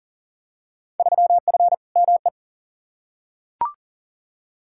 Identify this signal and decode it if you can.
Morse code, heard as a keyed tone, about 40 wpm, 700 Hz. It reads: 3F ME
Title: Field Day Exchanges - 40wpm